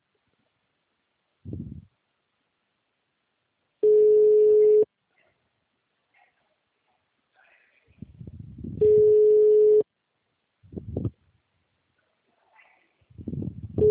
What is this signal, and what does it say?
Ringback tone (Europe, 425 Hz).
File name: out- -2013- - - .6773